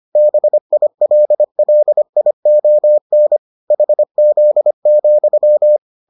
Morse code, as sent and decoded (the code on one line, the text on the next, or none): -... .. .-.. .-.. .. --- -. / .... --.. --..--
BILLION HZ,